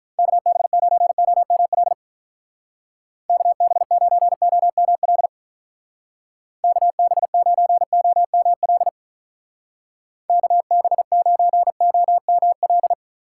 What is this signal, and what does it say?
Morse code, heard as a keyed tone, about 46 wpm, 700 Hz. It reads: KB9OML KB9OML KB9OML KB9OML